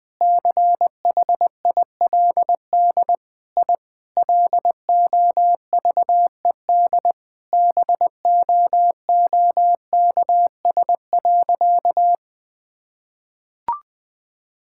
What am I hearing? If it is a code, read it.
Morse code, heard as a keyed tone, about 20 wpm, 700 Hz. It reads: CHILD I LOVED BOOKS.
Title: Sentences with 3-Word Phrases - Spoken Beginning - 20wpm